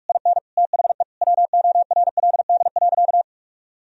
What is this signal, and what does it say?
Morse code, heard as a keyed tone, about 46 wpm, 700 Hz. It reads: IN THE WORLD.
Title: Sentences from Top 1000 Words - Review Part 1 - 45wpm